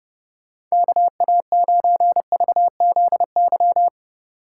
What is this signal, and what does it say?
Morse code, heard as a keyed tone, about 36 wpm, 700 Hz. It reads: KA9VZY
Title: Speed Racing - USA 2x3 Call Signs 30wpm